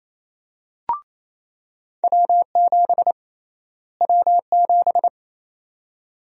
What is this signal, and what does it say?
Morse code, heard as a keyed tone, about 28 wpm, 700 Hz. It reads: W7 W7
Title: USA Call Sign Prefixes - Repeated 2x - 28wpm